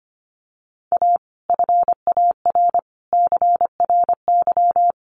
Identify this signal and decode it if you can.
Morse code, heard as a keyed tone, about 25 wpm, 700 Hz. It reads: A FAR CRY
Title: Sets of 5 Words Encore 25wpm